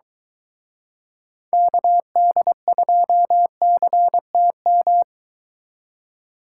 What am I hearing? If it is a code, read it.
Morse code, heard as a keyed tone, about 23 wpm, 700 Hz. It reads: KD2CTM